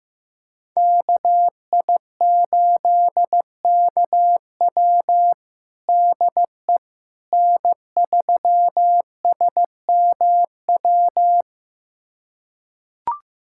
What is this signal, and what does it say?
Morse code, heard as a keyed tone, about 15 wpm, 700 Hz. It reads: KI8KW DE N3SMW
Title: USA Callsign DE Callsign - 5 Characters - 15wpm